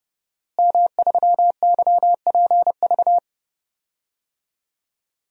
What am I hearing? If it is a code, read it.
Morse code, heard as a keyed tone, about 31 wpm, 700 Hz. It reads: M3YPV
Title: DXCC-100 Call Signs - 5 Characters - Repeated 2x - 30wpm